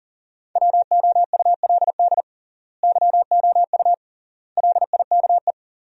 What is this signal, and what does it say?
Morse code, heard as a keyed tone, about 37 wpm, 700 Hz. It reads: WOULD YOU LIKE
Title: 3x Word-Spacing - Sentences from Top 100 Words - 40wpm